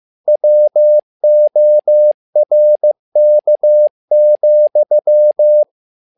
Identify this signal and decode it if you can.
Morse code, heard as a keyed tone, about 15 wpm, 600 Hz. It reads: WORK,
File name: learning center / tower trouble / tower trouble 15wpm